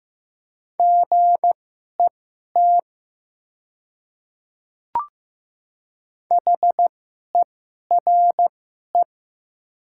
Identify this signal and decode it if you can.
Morse code, heard as a keyed tone, about 15 wpm, spaced as for 11 wpm, 700 Hz. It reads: GET HERE
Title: 2x Character Spacing - Top 200 Words - 15wpm